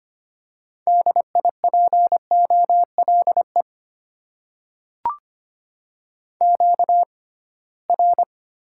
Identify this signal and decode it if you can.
Morse code, heard as a keyed tone, about 25 wpm, spaced as for 19 wpm, 690 Hz. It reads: DIPOLE QR
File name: Top 90 QSO Elements - 6x Character Spacing - 25wpm